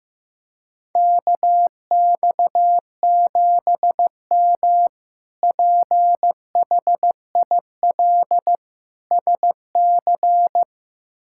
Morse code, heard as a keyed tone, about 15 wpm, 700 Hz. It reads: KX7M PHIL SC
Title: K1USN SST QSO Practice - 15wpm